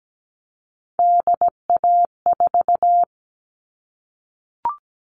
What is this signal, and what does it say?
Morse code, heard as a keyed tone, about 17 wpm, 700 Hz. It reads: DA4
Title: All Cut Numbers - 3-Digit 17wpm